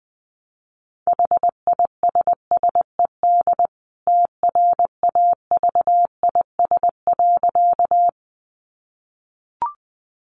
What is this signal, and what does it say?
Morse code, heard as a keyed tone, about 20 wpm, 700 Hz. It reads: HISSED TRAVIS.
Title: A Sound Of Thunder 20wpm